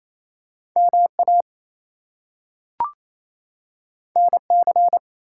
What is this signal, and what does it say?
Morse code, heard as a keyed tone, about 28 wpm, 700 Hz. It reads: MA NC
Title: US States and Canadian Provinces 28wpm